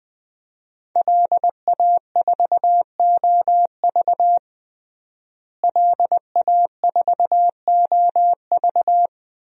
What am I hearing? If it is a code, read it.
Morse code, heard as a keyed tone, about 21 wpm, 700 Hz. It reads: LA4OV LA4OV
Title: DXCC-100 Call Signs - 5 Characters - Repeated 3x - 20wpm